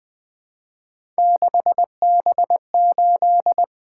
Morse code, heard as a keyed tone, about 20 wpm, 700 Hz. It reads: 6B8
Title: Three Letter-Number Combinations - 20wpm